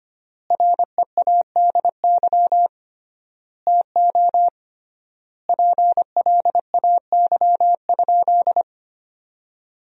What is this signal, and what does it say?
Morse code, heard as a keyed tone, about 25 wpm, 700 Hz. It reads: READY TO PLAY?
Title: 3x Word-Spacing - Sentences from Top 400 Words - 25wpm